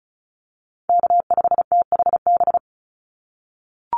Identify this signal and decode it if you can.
Morse code, heard as a keyed tone, about 34 wpm, 700 Hz. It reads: K5THB
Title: USA Call Signs 35wpm